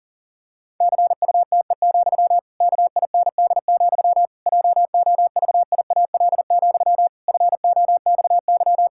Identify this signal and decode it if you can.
Morse code, heard as a keyed tone, about 37 wpm, 700 Hz. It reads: CUTE, KIND, JOVIAL, FOXY
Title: Pangram Challenge v2 - 40wpm